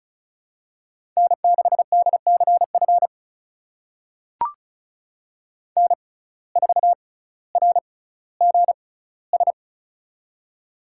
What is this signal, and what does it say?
Morse code, heard as a keyed tone, about 36 wpm, spaced as for 18 wpm, 700 Hz. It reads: N6DCF N4RGS
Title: USA Call Signs - 5 Characters - 6x Character Spacing - 35wpm